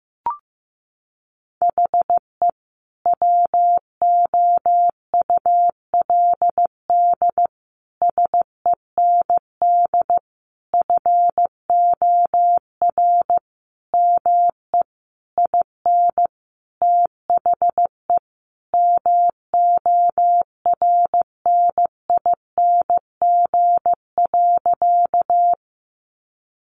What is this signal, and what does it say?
Morse code, heard as a keyed tone, about 15 wpm, 700 Hz. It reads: HE WOULD SEND FOR ME IN THE MORNING.
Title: Sentences from Top 1500 Words 15wpm